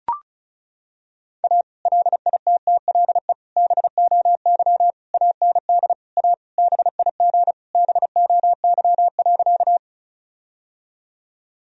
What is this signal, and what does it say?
Morse code, heard as a keyed tone, about 36 wpm, 700 Hz. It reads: A LITTLE BOY AND A BIG BOY.
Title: Sentences from Top 1000 Words - Review Part 1 - 35wpm